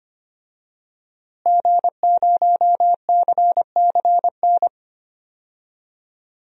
Morse code, heard as a keyed tone, about 25 wpm, 700 Hz. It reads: G0CCN
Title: DXCC-100 Call Signs - 5 Characters - Repeated 3x - 25wpm